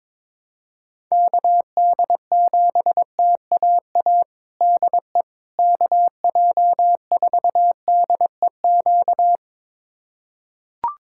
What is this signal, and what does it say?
Morse code, heard as a keyed tone, about 22 wpm, 700 Hz. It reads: KD7TAA DE KJ4DEQ
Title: USA Callsign DE Callsign 22wpm